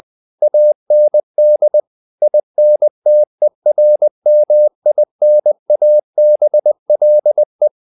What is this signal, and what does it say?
Morse code, heard as a keyed tone, about 20 wpm, 600 Hz. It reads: AND INTERMINABLE
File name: the old man and the sea part 2 20wpm